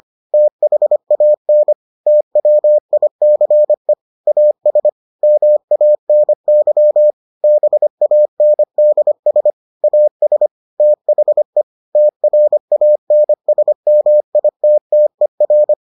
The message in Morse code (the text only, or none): THAN TWICE AS MANY BANDS AS THE TRANSMITTER